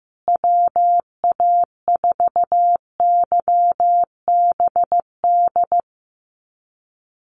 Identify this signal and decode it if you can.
Morse code, heard as a keyed tone, about 15 wpm, 700 Hz. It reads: WA4YBD
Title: USA Call Signs Repeated 3x 15wpm